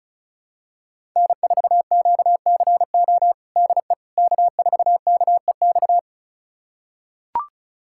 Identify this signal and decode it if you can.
Morse code, heard as a keyed tone, about 34 wpm, 700 Hz. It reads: N4QCO DE K4KEX
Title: USA Callsign DE Callsign 35wpm